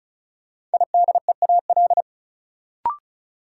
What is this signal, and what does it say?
Morse code, heard as a keyed tone, about 36 wpm, 700 Hz. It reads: IDEAL
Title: Sentences from Top 1400 Words - New Word Drill Alt - 35wpm